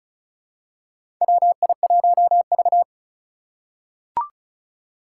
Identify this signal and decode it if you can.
Morse code, heard as a keyed tone, about 36 wpm, 700 Hz. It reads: WI1V
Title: USA Call Signs - 4 Characters - 4x Character Spacing - 35wpm